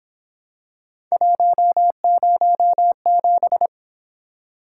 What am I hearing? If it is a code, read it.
Morse code, heard as a keyed tone, about 25 wpm, 700 Hz. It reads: 107
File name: Speed-Racing - Three Letter-Number Combinations - 22wpm